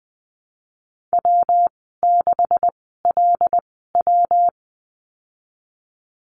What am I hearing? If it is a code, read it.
Morse code, heard as a keyed tone, about 20 wpm, spaced as for 15 wpm, 700 Hz. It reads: W6LW
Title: 173 Lesson 28g - Number 9 - Call Signs 20wpm